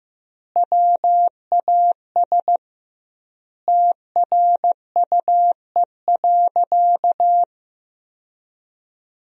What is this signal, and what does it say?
Morse code, heard as a keyed tone, about 15 wpm, 700 Hz. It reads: WAS TRUE.
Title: 2x Word-Spacing - Sentences from Top 400 Words - 15wpm